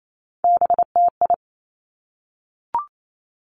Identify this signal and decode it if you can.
Morse code, heard as a keyed tone, about 28 wpm, 690 Hz. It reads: BTI
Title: Three Challenging Letter-Number Combinations 28wpm